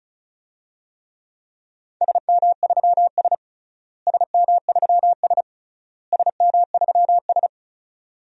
Morse code, heard as a keyed tone, about 36 wpm, 700 Hz. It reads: SM3S SM3S SM3S
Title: DXCC-100 Call Signs - 4 Characters - Repeated 3x - 35wpm